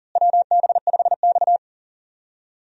Morse code, heard as a keyed tone, about 40 wpm, 700 Hz. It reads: WB5X
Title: USA Call Signs - 4 Characters - Repeated 3x - 40wpm